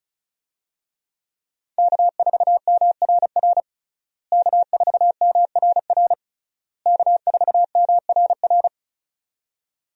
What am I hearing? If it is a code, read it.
Morse code, heard as a keyed tone, about 35 wpm, 700 Hz. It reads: K4MRR K4MRR K4MRR
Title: USA Call Signs - 5 Characters - Repeated 3x - 35wpm